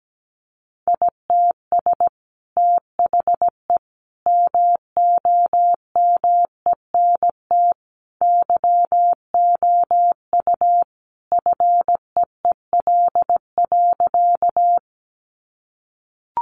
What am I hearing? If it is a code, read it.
Morse code, heard as a keyed tone, about 17 wpm, 700 Hz. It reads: ITS THE MOMENT YOU FEEL.
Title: Sentences from Top 1000 Words 17wpm